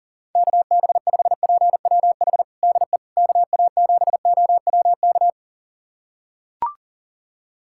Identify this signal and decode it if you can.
Morse code, heard as a keyed tone, about 40 wpm, 700 Hz. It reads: KB5PWH DE KA7OWK
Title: USA Callsign DE Callsign 40wpm